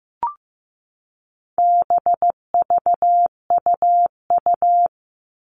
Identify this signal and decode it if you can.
Morse code, heard as a keyed tone, about 15 wpm, 700 Hz. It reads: BVUU
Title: All Cut Numbers - 4-Digit 15wpm